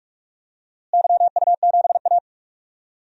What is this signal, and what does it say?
Morse code, heard as a keyed tone, about 45 wpm, 700 Hz. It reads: YU7A
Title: DXCC-100 Call Signs - 4 Characters - Repeated 3x - 45wpm